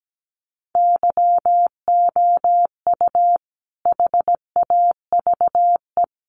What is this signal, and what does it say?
Morse code, heard as a keyed tone, about 17 wpm, 700 Hz. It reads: YOU HAVE
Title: Sentences from Top 700 Words 17wpm